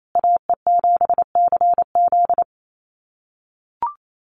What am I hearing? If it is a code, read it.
Morse code, heard as a keyed tone, about 28 wpm, 700 Hz. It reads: AE7CZ
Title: USA Call Signs Repeated 3x 28wpm